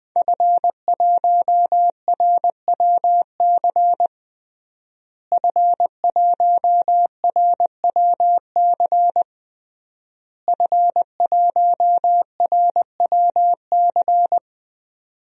Morse code, heard as a keyed tone, about 21 wpm, 700 Hz. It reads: F1RWC F1RWC F1RWC
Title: DXCC-100 Call Signs - 5 Characters - Repeated 3x - 20wpm